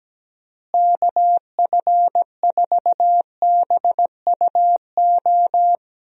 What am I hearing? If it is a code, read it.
Morse code, heard as a keyed tone, about 17 wpm, 700 Hz. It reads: KF4BUO
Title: USA Call Signs - 6 Characters - Repeated 3x - 17wpm